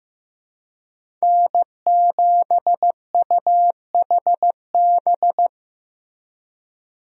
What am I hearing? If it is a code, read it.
Morse code, heard as a keyed tone, about 15 wpm, 700 Hz. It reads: N7UHB
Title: USA Call Signs - 5 Characters - Repeated 3x - 15wpm